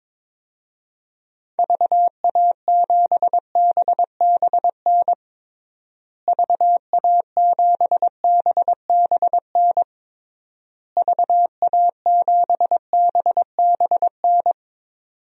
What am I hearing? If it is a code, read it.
Morse code, heard as a keyed tone, about 21 wpm, 700 Hz. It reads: VA7BBN VA7BBN VA7BBN
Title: DXCC-100 Call Signs - 6 Characters - Repeated 3x - 22wpm